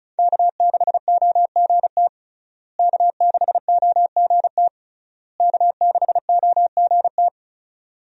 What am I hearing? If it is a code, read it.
Morse code, heard as a keyed tone, about 36 wpm, 700 Hz. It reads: K6OGT K6OGT K6OGT
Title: USA Call Signs - 5 Characters - Repeated 3x - 35wpm